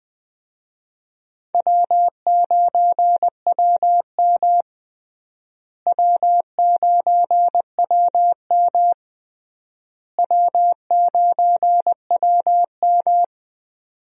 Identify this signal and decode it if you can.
Morse code, heard as a keyed tone, about 20 wpm, 700 Hz. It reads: W9WM W9WM W9WM
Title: USA Call Signs - 4 Characters - Repeated 3x - 20wpm